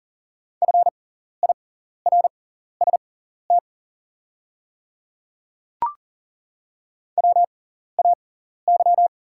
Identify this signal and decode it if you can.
Morse code, heard as a keyed tone, about 40 wpm, spaced as for 14 wpm, 690 Hz. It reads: FIRST WAY
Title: 6x Character Spacing - Top 200 Words - 40wpm